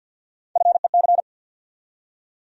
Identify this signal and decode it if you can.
Morse code, heard as a keyed tone, about 48 wpm, 700 Hz. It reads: FEC
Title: USA Call Sign Postfixes - 3 Letters 50wpm